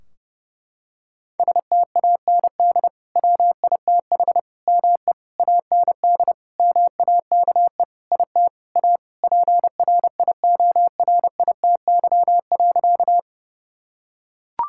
Morse code, heard as a keyed tone, about 30 wpm, 700 Hz. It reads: STAND WITH ME AND MAKE IT A PRIORITY.
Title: Sentences from Top 1600 Words 30wpm